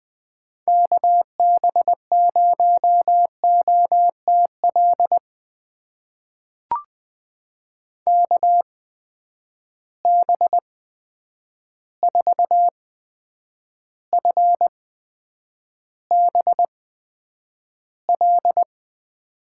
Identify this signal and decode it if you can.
Morse code, heard as a keyed tone, about 20 wpm, spaced as for 9 wpm, 700 Hz. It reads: KB0OTL KB4FBL
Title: USA Call Signs - 8x Character Spacing - 20wpm